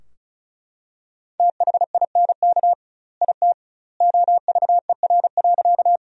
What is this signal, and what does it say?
Morse code, heard as a keyed tone, about 34 wpm, 700 Hz. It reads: THINK IT OVER.
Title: 2x Word-Spacing - Sentences from Top 200 Words - 35wpm